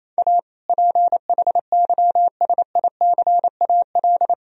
Morse code, heard as a keyed tone, about 28 wpm, 700 Hz. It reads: A PHYSICAL
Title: Sentences from Top 1000 Words - Review Part 4 - 28wpm